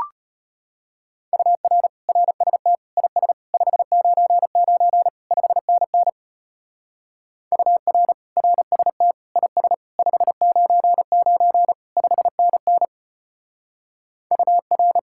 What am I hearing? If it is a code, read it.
Morse code, heard as a keyed tone, about 33 wpm, 700 Hz. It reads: UR RST IS 599 5NN UR RST IS 599 5NN UR